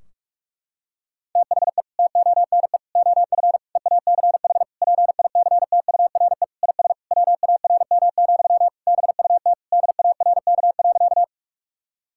Morse code, heard as a keyed tone, about 46 wpm, 700 Hz. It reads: THE TONE OF EACH PICTURE IS WARM, BUT DARK.
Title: Sentences from Top 1600 Words 45wpm